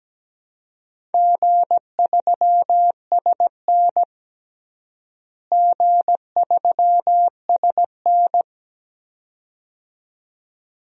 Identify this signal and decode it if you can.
Morse code, heard as a keyed tone, about 17 wpm, 700 Hz. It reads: G3SN G3SN
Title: DXCC-100 Call Signs - 4 Characters - Repeated 2x - 17wpm